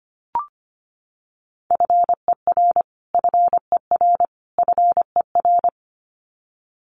Morse code, heard as a keyed tone, about 25 wpm, 700 Hz. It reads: FER FER FER
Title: Abbreviations Repeated 3x - 25wpm